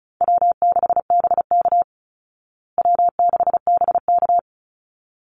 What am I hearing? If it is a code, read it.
Morse code, heard as a keyed tone, about 34 wpm, 700 Hz. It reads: W6BK W6BK
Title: Call Signs Repeated 3x - 35wpm